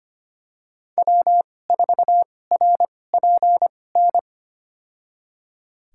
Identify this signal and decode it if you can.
Morse code, heard as a keyed tone, about 25 wpm, spaced as for 18 wpm, 700 Hz. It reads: W4RPN
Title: USA Call Signs - 5 Characters - 2x Character Spacing - 25wpm